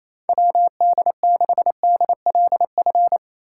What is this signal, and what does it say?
Morse code, heard as a keyed tone, about 28 wpm, 700 Hz. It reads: WD6DLF
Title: USA Call Signs - 6 Characters - 4x Character Spacing - 28wpm